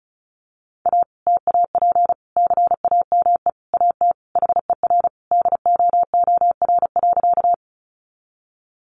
Morse code, heard as a keyed tone, about 34 wpm, 700 Hz. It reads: A TAP CAME AT HER DOOR.
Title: Sentences from Top 1400 Words 35wpm